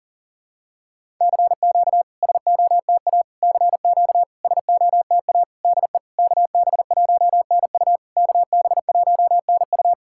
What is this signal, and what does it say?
Morse code, heard as a keyed tone, about 44 wpm, 700 Hz. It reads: CQ SOTA CQ SOTA DE KB1NU KB1NU
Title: SOTA QSO Practice 40wpm